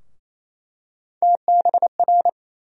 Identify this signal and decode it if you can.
Morse code, heard as a keyed tone, about 27 wpm, 700 Hz. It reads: TBR
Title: USA Call Sign Postfixes - 3 Letters 28wpm